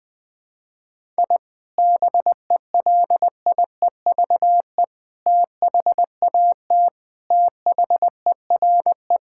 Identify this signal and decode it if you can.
Morse code, heard as a keyed tone, about 20 wpm, 700 Hz. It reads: I BELIEVE THAT THERE
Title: Sentences with 3-Word Phrases - Spoken Beginning - 20wpm